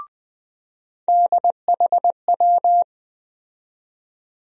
Morse code, heard as a keyed tone, about 21 wpm, 700 Hz. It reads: DHW
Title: USA Call Sign Postfixes - 3 Letters 20wpm